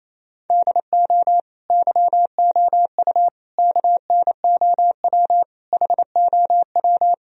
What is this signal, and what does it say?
Morse code, heard as a keyed tone, about 28 wpm, 700 Hz. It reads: DO YOU KNOW HOW
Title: Sentences from Top 1000 Words - Review Part 3 - 28wpm